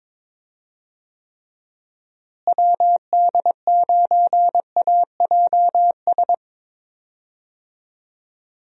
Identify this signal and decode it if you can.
Morse code, heard as a keyed tone, about 22 wpm, 700 Hz. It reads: WD9AJS
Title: USA Call Signs - 6 Characters - Repeated 3x - 22wpm